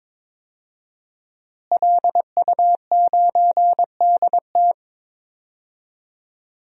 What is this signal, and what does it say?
Morse code, heard as a keyed tone, about 21 wpm, 700 Hz. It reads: LU9DT
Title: DXCC-100 Call Signs - 5 Characters - Repeated 2x - 22wpm